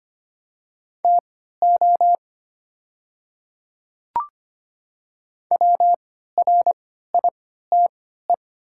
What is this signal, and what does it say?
Morse code, heard as a keyed tone, about 25 wpm, spaced as for 14 wpm, 700 Hz. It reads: TO WRITE
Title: 3x Character Spacing - Top 100 Words - 25wpm